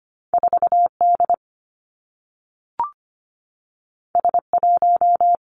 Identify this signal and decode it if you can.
Morse code, heard as a keyed tone, about 25 wpm, 700 Hz. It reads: 4 D S 1
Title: Two Challenging Letter-Number Combinations 25wpm